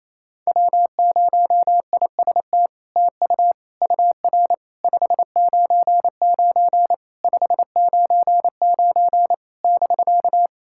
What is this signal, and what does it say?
Morse code, heard as a keyed tone, about 28 wpm, 700 Hz. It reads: W0IST TU UR 599 599 <BK>
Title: SOTA QSO Practice 28wpm